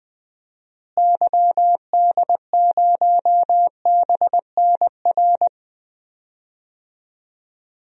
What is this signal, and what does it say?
Morse code, heard as a keyed tone, about 20 wpm, 700 Hz. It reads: YD0BNR
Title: DXCC-100 Call Signs - 6 Characters - Repeated 3x - 20wpm